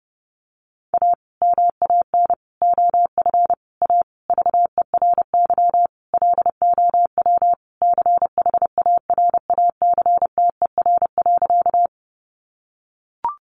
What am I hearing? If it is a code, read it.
Morse code, heard as a keyed tone, about 30 wpm, 700 Hz. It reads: A MAN OF A VERY LOW CHARACTER.
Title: Sentences from Top 1200 Words 30wpm